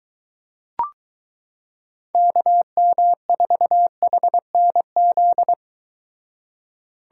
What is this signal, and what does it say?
Morse code, heard as a keyed tone, about 22 wpm, 700 Hz. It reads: KM4HNZ